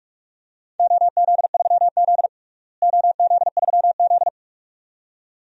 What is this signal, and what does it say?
Morse code, heard as a keyed tone, about 43 wpm, 700 Hz. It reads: OZ3Z OZ3Z
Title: DXCC-100 Call Signs - 4 Characters - Repeated 2x - 45wpm